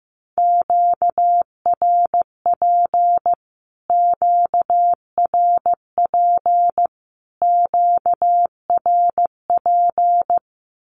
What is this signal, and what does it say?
Morse code, heard as a keyed tone, about 15 wpm, 700 Hz. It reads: QRP QRP QRP
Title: Q-Codes Repeated 3x 15wpm